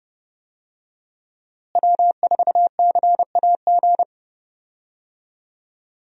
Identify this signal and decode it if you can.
Morse code, heard as a keyed tone, about 30 wpm, 700 Hz. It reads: W4CAG
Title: USA Call Signs - 5 Characters - Repeated 2x - 30wpm